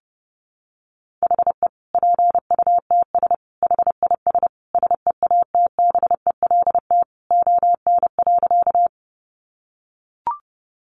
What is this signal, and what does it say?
Morse code, heard as a keyed tone, about 30 wpm, 700 Hz. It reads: HE PUTS HIS SEATBELT ON.
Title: Additional Easy Phrases 30wpm